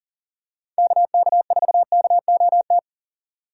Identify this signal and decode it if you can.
Morse code, heard as a keyed tone, about 40 wpm, 700 Hz. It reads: KK4KOT
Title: USA Call Signs - 6 Characters - 2x Character Spacing - 40wpm